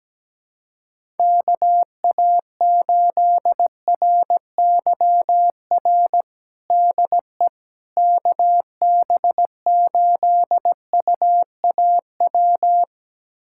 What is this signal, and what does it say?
Morse code, heard as a keyed tone, about 17 wpm, 700 Hz. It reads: KA8RYR DE KB8UAW
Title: USA Callsign DE Callsign 17wpm